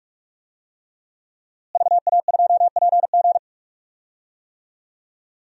Morse code, heard as a keyed tone, about 47 wpm, 700 Hz. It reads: VA2PG
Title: DXCC-100 Call Signs - 5 Characters - Repeated 3x - 45wpm